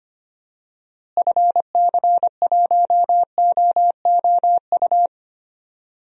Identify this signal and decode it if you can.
Morse code, heard as a keyed tone, about 25 wpm, 700 Hz. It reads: FC1OOU
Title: DXCC-100 Call Signs - 6 Characters - Repeated 3x - 25wpm